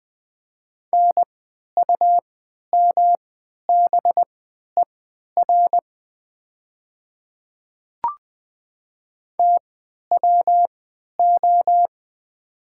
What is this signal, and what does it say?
Morse code, heard as a keyed tone, about 20 wpm, spaced as for 11 wpm, 700 Hz. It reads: NUMBER TWO
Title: 3x Character Spacing - Top 100 Words - 20wpm